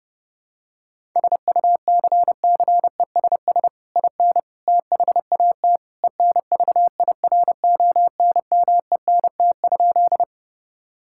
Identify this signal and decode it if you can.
Morse code, heard as a keyed tone, about 32 wpm, 700 Hz. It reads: SUCCESS IN THAT ENVIRONMENT?
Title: Sentences from Top 1000 Words - Review Part 5 - 30wpm